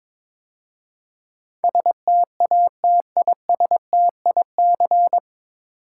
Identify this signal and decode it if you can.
Morse code, heard as a keyed tone, about 21 wpm, 700 Hz. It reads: STATISTIC
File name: 9 Letter Words - First 3 Letters Given - 22wpm